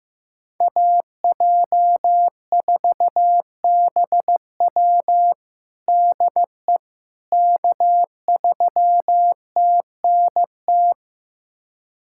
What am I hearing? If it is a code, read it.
Morse code, heard as a keyed tone, about 15 wpm, 700 Hz. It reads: AJ4BW DE K3TNT
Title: USA Callsign DE Callsign - 5 Characters - 15wpm